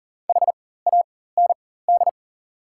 Morse code, heard as a keyed tone, about 39 wpm, spaced as for 18 wpm, 690 Hz. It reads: HAND
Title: 4x Character Spacing - Top 500 Words - 40wpm